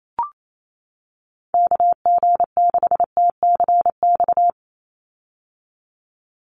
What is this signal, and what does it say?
Morse code, heard as a keyed tone, about 28 wpm, 700 Hz. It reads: KG6TCX
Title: USA Call Signs 28wpm